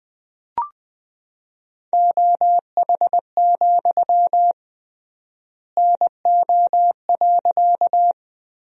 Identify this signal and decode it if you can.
Morse code, heard as a keyed tone, about 20 wpm, 700 Hz. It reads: OH, NO.
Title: 3x Word-Spacing - Sentences from Top 500 Words - 20wpm